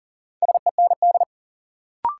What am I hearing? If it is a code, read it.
Morse code, heard as a keyed tone, about 40 wpm, 700 Hz. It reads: SEND
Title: Sentences from Top 1500 Words - New Word Drill - 40wpm